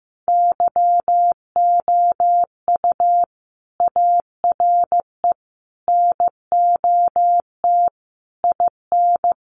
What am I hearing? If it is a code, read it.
Morse code, heard as a keyed tone, about 15 wpm, 700 Hz. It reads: YOU ARE NOT IN
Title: Sentences from Top 1300 Words 15wpm